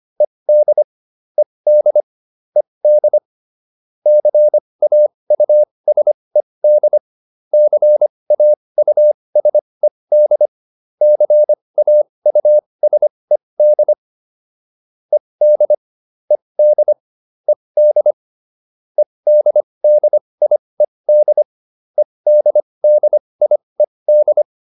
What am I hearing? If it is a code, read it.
Morse code, heard as a keyed tone, about 25 wpm, spaced as for 20 wpm, 600 Hz. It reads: ED ED ED CAUSED CAUSED CAUSED ED ED ED EDDIED EDDIED